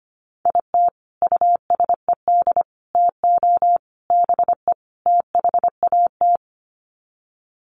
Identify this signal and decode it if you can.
Morse code, heard as a keyed tone, about 25 wpm, 700 Hz. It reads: IT USED TO BE THAT
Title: Sets of 5 Words Encore2 25wpm